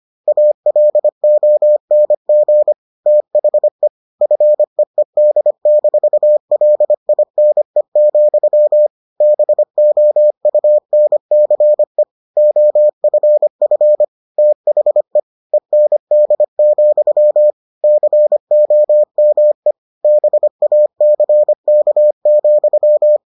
Morse code, heard as a keyed tone, about 25 wpm, 600 Hz. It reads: ALONG THE FEED-LINE, BOUNCE OFF THE END, COME BACK,